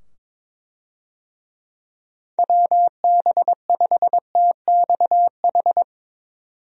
Morse code, heard as a keyed tone, about 22 wpm, 700 Hz. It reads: WB5TXH
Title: USA Call Signs - 6 Characters - 22wpm